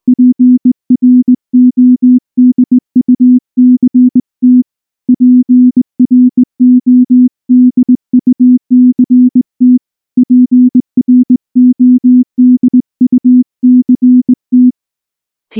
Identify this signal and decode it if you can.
Morse code, heard as a keyed tone, about 20 wpm, 250 Hz. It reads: PRODUCT PRODUCT PRODUCT
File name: morse / morse-training-20WPM-4